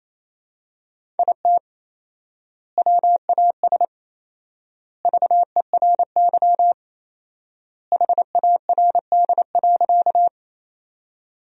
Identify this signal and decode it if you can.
Morse code, heard as a keyed tone, about 28 wpm, 700 Hz. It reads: IT WAS VERY HARD.
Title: 4x Word-Spacing - Sentences from Top 300 Words - 28wpm